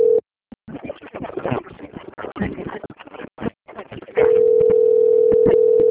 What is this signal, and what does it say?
Ringback tone (North America), 440 + 480 Hz.
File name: out- - - - - .26504